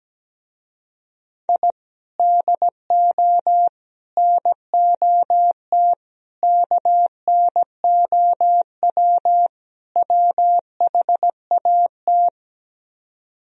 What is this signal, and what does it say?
Morse code, heard as a keyed tone, about 17 wpm, 700 Hz. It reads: I DO NOT KNOW WHAT
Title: Speed-Racing - Sets of 5 Words 17wpm